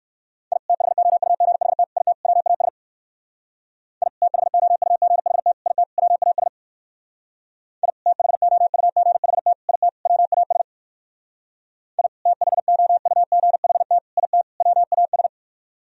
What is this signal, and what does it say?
Morse code, heard as a keyed tone, about 57 wpm, 700 Hz. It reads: I THOUGHT IT WAS I THOUGHT IT WAS I THOUGHT IT WAS I THOUGHT IT WAS